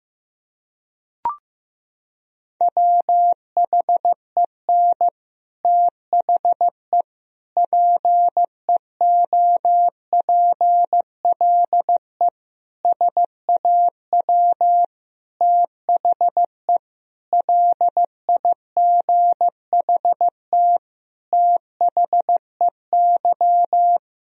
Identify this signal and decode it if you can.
Morse code, heard as a keyed tone, about 15 wpm, 700 Hz. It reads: WHEN THE PEOPLE SAW THE LIGHT THEY
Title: Sentences from Top 1000 Words - Review Part 1 - 15wpm